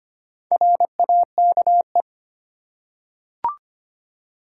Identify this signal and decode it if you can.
Morse code, heard as a keyed tone, about 25 wpm, 700 Hz. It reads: RAKE
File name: practice / 25wpm / Easy Words with Hints - 25wpm